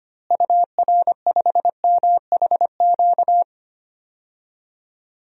Morse code, heard as a keyed tone, about 25 wpm, 700 Hz. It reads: UR5MHQ
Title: DXCC-100 Call Signs - 6 Characters - Repeated 3x - 25wpm